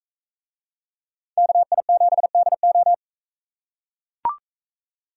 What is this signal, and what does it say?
Morse code, heard as a keyed tone, about 44 wpm, 700 Hz. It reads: KI7DO